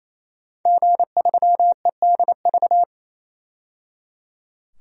Morse code, heard as a keyed tone, about 28 wpm, 700 Hz. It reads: G3EDV
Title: DXCC-100 Call Signs - 5 Characters - 28wpm